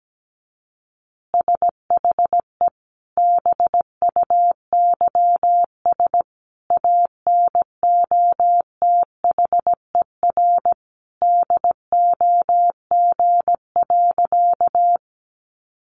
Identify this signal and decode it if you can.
Morse code, heard as a keyed tone, about 17 wpm, 700 Hz. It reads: SHE BUYS ANOTHER DOG.
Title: Easy Phrases 17wpm